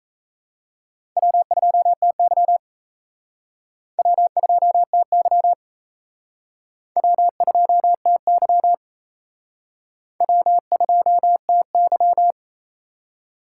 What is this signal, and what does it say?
Morse code, heard as a keyed tone, about 37 wpm, 700 Hz. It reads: W2TY W2TY W2TY W2TY